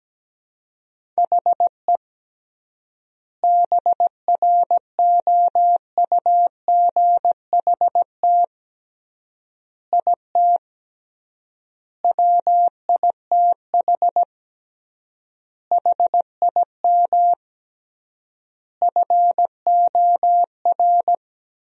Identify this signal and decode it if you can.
Morse code, heard as a keyed tone, about 17 wpm, 700 Hz. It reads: HE BROUGHT IT WITH HIM FOR
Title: 3x Word-Spacing - Sentences from Top 500 Words - 17wpm